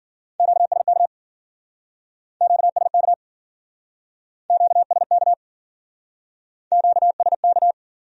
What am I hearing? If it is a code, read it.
Morse code, heard as a keyed tone, about 52 wpm, 700 Hz. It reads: QSK QSK QSK QSK